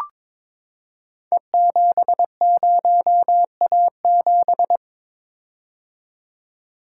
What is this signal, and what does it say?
Morse code, heard as a keyed tone, about 22 wpm, 690 Hz. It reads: E70A7
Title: Code Groups - 5 Letter:Numbers - 22wpm